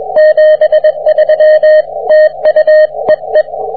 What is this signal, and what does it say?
Morse code, heard as a keyed tone, about 24 wpm, 620 Hz, with radio noise. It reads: 73 TU EE